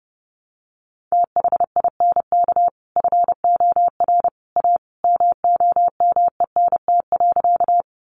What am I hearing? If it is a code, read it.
Morse code, heard as a keyed tone, about 30 wpm, 700 Hz. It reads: THINK FOR A MOMENT.
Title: Sentences from Top 1000 Words 30wpm